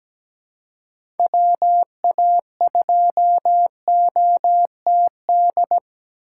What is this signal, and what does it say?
Morse code, heard as a keyed tone, about 17 wpm, 700 Hz. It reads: WA2OTD
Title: USA Call Signs - 6 Characters - 3x Character Spacing - 17wpm